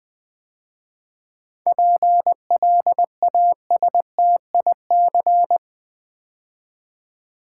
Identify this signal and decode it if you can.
Morse code, heard as a keyed tone, about 20 wpm, 700 Hz. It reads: PLASTIC
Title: 7 Letter Words - First 3 Letters Given - 20wpm